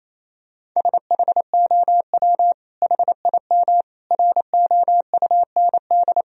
Morse code, heard as a keyed tone, about 27 wpm, 700 Hz. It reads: SHOW HIM ROUND
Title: Sentences from Top 1000 Words - Review Part 1 - 28wpm